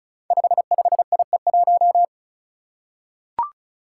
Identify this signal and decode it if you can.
Morse code, heard as a keyed tone, about 36 wpm, 700 Hz. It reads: 55IE1
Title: Five Challenging Letter-Number Combinations 35wpm